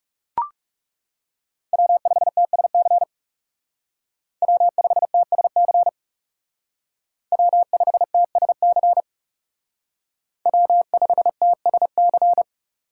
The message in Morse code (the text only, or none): W5TSC W5TSC W5TSC W5TSC